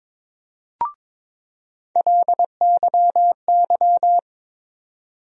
Morse code, heard as a keyed tone, about 22 wpm, 700 Hz. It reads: LYY
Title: USA Call Sign Postfixes - 3 Letters 22wpm